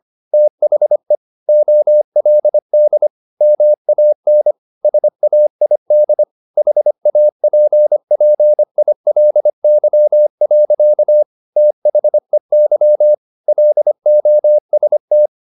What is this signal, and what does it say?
Morse code, heard as a keyed tone, about 25 wpm, 600 Hz. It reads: THE OLD MAN SAID HAPPILY. THEY LOST
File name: the old man and the sea part 1 25wpm